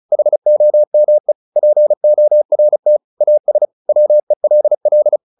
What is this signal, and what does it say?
Morse code, heard as a keyed tone, about 35 wpm, 600 Hz. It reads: HOME PORT AS WELL
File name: the old man and the sea 2 35wpm